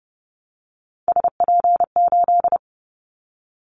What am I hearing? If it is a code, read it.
Morse code, heard as a keyed tone, about 30 wpm, 700 Hz. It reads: SP8
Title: Callsign Prefixes for Top 25 DXCC - Repeated 3x - 30wpm